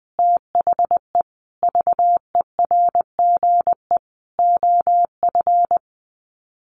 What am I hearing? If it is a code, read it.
Morse code, heard as a keyed tone, about 20 wpm, 700 Hz. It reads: THE VERGE OF
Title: Sets of 4 Words 20wpm